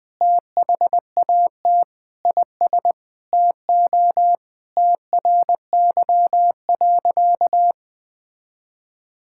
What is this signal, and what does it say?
Morse code, heard as a keyed tone, about 20 wpm, 700 Hz. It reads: THAT IS TO TRY.
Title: Sentences with 8-Word Phrases - Spoken Beginning - 20wpm